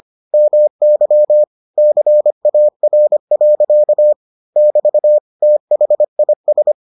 Morse code, heard as a keyed tone, about 25 wpm, 600 Hz. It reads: MY CAR. <BT> THIS